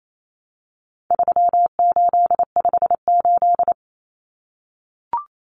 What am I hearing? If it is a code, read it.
Morse code, heard as a keyed tone, about 28 wpm, 700 Hz. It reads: 3858
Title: Common Cut Numbers - 4-Digit 28wpm